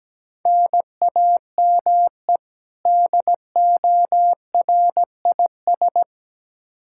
Morse code, heard as a keyed tone, about 17 wpm, 700 Hz. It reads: NAME DORIS
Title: Practice QSO - Exchange 1 - Beginning - 17wpm